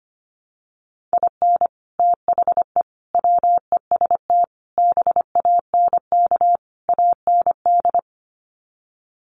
Morse code, heard as a keyed tone, about 25 wpm, 700 Hz. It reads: IN THE WEST BANK AND
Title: Sets of 5 Words Encore2 25wpm